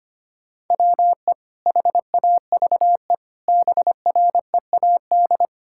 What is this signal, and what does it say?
Morse code, heard as a keyed tone, about 25 wpm, 700 Hz. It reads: WE HAVE BREAD
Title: Sentences from Top 1600 Words 25wpm